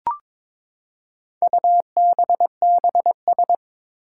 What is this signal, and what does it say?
Morse code, heard as a keyed tone, about 22 wpm, 700 Hz. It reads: UBBS
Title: Four Challenging Letter-Number Combinations 22wpm